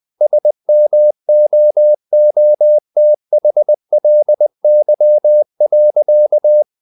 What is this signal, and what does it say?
Morse code, heard as a keyed tone, about 20 wpm, 600 Hz. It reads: SMOOTHLY.
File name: the old man and the sea part 2 20wpm